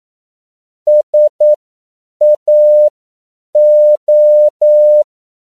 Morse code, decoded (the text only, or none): SAO